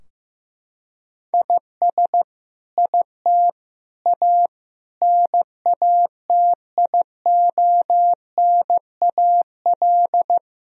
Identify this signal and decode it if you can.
Morse code, heard as a keyed tone, about 16 wpm, 700 Hz. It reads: IS IT A NATIONAL
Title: Sentences from Top 1000 Words - Review Part 4 - 15wpm